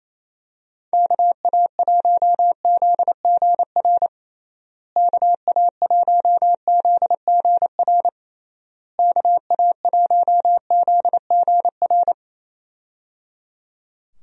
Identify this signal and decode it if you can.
Morse code, heard as a keyed tone, about 28 wpm, 700 Hz. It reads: KA1ZGR KA1ZGR KA1ZGR
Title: USA Call Signs - 6 Characters - Repeated 3x - 28wpm